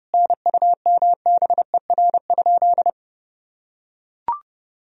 Morse code, heard as a keyed tone, about 32 wpm, 700 Hz. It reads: NUMBER?
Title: Sentences from Top 1000 Words - Review Part 1 - 30wpm